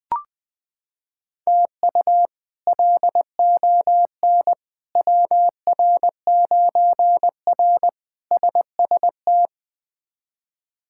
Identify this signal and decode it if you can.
Morse code, heard as a keyed tone, about 21 wpm, 700 Hz. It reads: TU LON WR9R SST
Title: K1USN SST QSO Practice - 20wpm